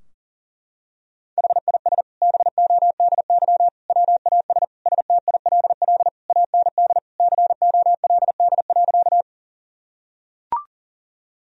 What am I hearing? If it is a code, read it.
Morse code, heard as a keyed tone, about 40 wpm, 700 Hz. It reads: HIS BODY WAS STILL AND COLD.
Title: Sentences from Top 1000 Words - Review Part 2 - 40wpm